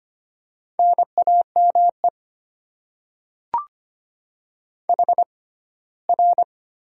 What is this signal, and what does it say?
Morse code, heard as a keyed tone, about 25 wpm, spaced as for 17 wpm, 700 Hz. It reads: NAME HR
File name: Top 90 QSO Elements - 6x Character Spacing - 25wpm